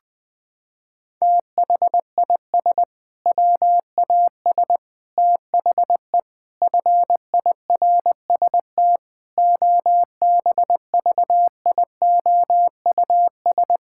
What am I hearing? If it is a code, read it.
Morse code, heard as a keyed tone, about 21 wpm, 700 Hz. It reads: THIS WAS THE FIRST OBVIOUS
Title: Sentences from Top 1600 Words 20wpm